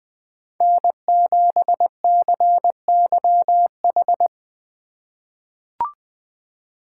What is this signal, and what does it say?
Morse code, heard as a keyed tone, about 20 wpm, 700 Hz. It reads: N7CYH
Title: USA Call Signs - 5 Characters - 4x Character Spacing - 20wpm